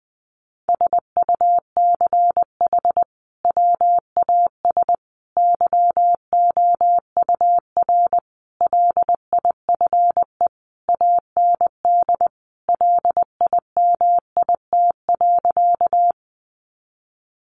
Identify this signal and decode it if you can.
Morse code, heard as a keyed tone, about 20 wpm, 700 Hz. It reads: SUCH WAS YOUR LIFE AND LIMIT.
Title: Sentences from Top 1500 Words 20wpm